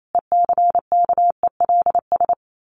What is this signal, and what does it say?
Morse code, heard as a keyed tone, about 28 wpm, 700 Hz. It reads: ECKELS
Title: A Sound Of Thunder 28wpm